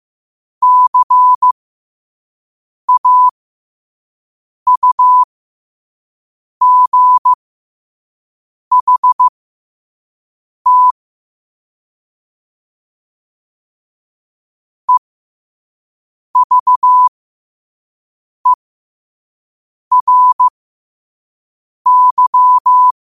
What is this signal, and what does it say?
Morse code, heard as a keyed tone, about 15 wpm, spaced as for 5 wpm, 1000 Hz. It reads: CAUGHT EVERY
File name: 5wpm